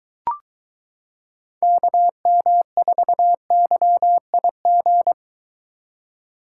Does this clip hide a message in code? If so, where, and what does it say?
Yes: Morse code from 1.6 to 5.1 s, heard as a keyed tone, about 22 wpm, 700 Hz. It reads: KM4YIG